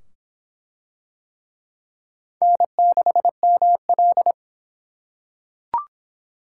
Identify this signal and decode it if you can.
Morse code, heard as a keyed tone, about 25 wpm, 700 Hz. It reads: N6ML